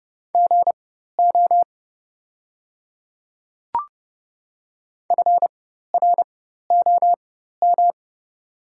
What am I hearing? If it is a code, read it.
Morse code, heard as a keyed tone, about 30 wpm, spaced as for 14 wpm, 700 Hz. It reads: GO FROM
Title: 4x Character Spacing - Top 200 Words - 30wpm